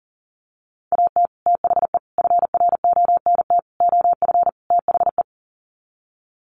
Morse code, heard as a keyed tone, about 40 wpm, 700 Hz. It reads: AT THE FRONT OF THE
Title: Sets of 5 Words 40wpm